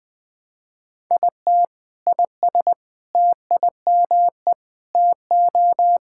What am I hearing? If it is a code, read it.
Morse code, heard as a keyed tone, about 20 wpm, 700 Hz. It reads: IT IS TIME TO
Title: Sentences with 7-Word Phrases - 20wpm